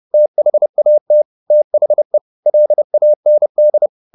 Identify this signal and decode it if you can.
Morse code, heard as a keyed tone, about 30 wpm, 600 Hz. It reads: THAT THE LAND
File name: the old man and the sea / the old man and the sea 1 30wpm